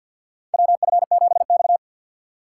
Morse code, heard as a keyed tone, about 52 wpm, 700 Hz. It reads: WF7X
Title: USA Call Signs - 3x Character Spacing - 50wpm